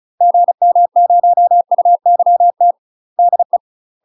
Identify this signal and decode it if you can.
Morse code, heard as a keyed tone, about 34 wpm, 700 Hz. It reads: GM0UYT DE
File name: UK MASTER QSOs TRAINING / 35WPM / UK MASTER QSOs 0003